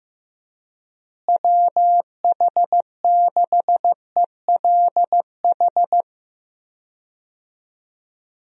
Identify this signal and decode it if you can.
Morse code, heard as a keyed tone, about 15 wpm, 700 Hz. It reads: WH6ELH
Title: USA Call Signs - 6 Characters - Repeated 2x - 15wpm